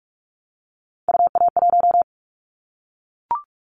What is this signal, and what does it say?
Morse code, heard as a keyed tone, about 44 wpm, 700 Hz. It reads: UA1
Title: Callsign Prefixes for Top 25 DXCC - 45wpm